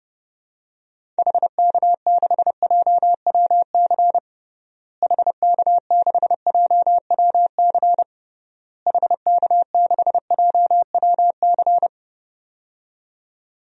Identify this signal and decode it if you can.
Morse code, heard as a keyed tone, about 32 wpm, 700 Hz. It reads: HK6JWC HK6JWC HK6JWC
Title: DXCC-100 Call Signs - 6 Characters - Repeated 3x - 30wpm